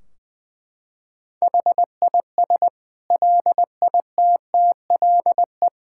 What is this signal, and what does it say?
Morse code, heard as a keyed tone, about 20 wpm, 700 Hz. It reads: HIS LITTLE
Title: Sentences from Top 1600 Words 20wpm